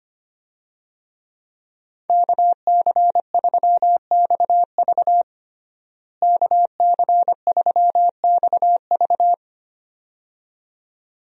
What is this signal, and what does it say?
Morse code, heard as a keyed tone, about 25 wpm, 700 Hz. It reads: KC3XV KC3XV
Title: USA Call Signs - 5 Characters - Repeated 2x - 25wpm